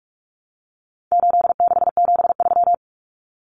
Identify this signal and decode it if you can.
Morse code, heard as a keyed tone, about 44 wpm, 700 Hz. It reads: 8673
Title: Common Cut Numbers - 4-Digit 45wpm